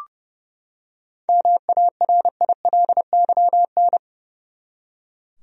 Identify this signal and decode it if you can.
Morse code, heard as a keyed tone, about 30 wpm, 700 Hz. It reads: MARILYN
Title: USA Common Names - 30wpm